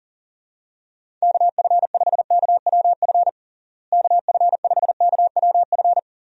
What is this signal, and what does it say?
Morse code, heard as a keyed tone, about 40 wpm, 700 Hz. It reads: KF5KWF KF5KWF
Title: USA Call Signs - 6 Characters - Repeated 2x - 40wpm